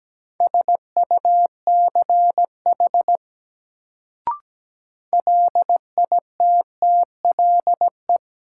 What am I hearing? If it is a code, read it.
Morse code, heard as a keyed tone, about 17 wpm, 700 Hz. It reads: SUCH LITTLE
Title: Sentences from Top 200 Words - New Word Drill - 17wpm